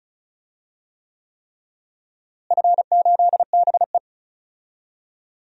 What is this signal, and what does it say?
Morse code, heard as a keyed tone, about 36 wpm, 700 Hz. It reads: F8BE
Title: DXCC-100 Call Signs - 4 Characters - Repeated 3x - 35wpm